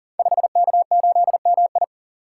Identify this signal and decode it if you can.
Morse code, heard as a keyed tone, about 38 wpm, 700 Hz. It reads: 5K8MI
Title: Code Groups - 5 Letter:Numbers - 40wpm